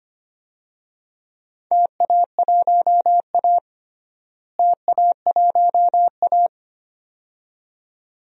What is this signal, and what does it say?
Morse code, heard as a keyed tone, about 25 wpm, 700 Hz. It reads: TA1A TA1A
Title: DXCC-100 Call Signs - 4 Characters - Repeated 2x - 25wpm